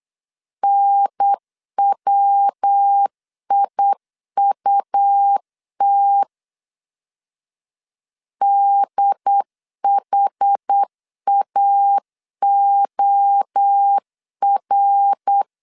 Morse code, heard as a keyed tone, about 8 wpm, 800 Hz. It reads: NWIUT DHAOR
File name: CW / DET / 1 3 1500C S B 8wpm 4141